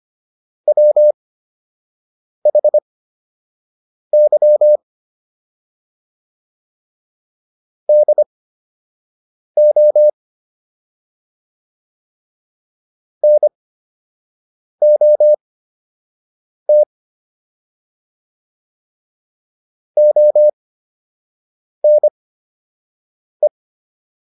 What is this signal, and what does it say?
Morse code, heard as a keyed tone, about 25 wpm, spaced as for 6 wpm, 600 Hz. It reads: WHY DO NOT ONE